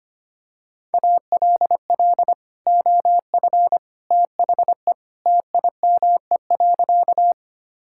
Morse code, heard as a keyed tone, about 25 wpm, 700 Hz. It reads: ALL OF THE TIME.
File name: practice / 25wpm / Sentences with 3-Word Phrases - 25wpm